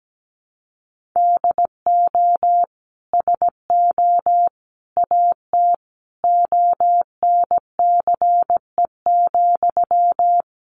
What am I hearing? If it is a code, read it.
Morse code, heard as a keyed tone, about 17 wpm, 700 Hz. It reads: DO SO AT ONCE,
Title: Sentences from Top 1500 Words 17wpm